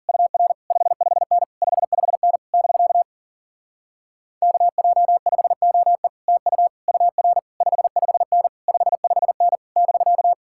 Morse code, heard as a keyed tone, about 52 wpm, 700 Hz. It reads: UR 55N 55N <BK> KJ5OE TU UR 55N 55N <BK>